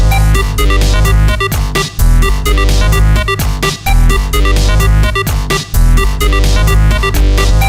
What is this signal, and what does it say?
There is no code.